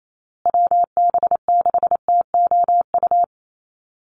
Morse code, heard as a keyed tone, about 28 wpm, 700 Hz. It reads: WB6TOU
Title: Call Signs Repeated 2x - 28wpm